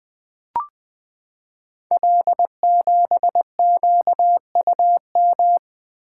Morse code, heard as a keyed tone, about 20 wpm, 700 Hz. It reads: L7QUM
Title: Code Groups - 5 Letter:Numbers - 20wpm